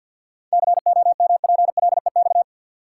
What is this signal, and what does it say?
Morse code, heard as a keyed tone, about 53 wpm, 700 Hz. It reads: COMPLEX
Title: Sentences from Top 900 Words - New Word Drill - 50wpm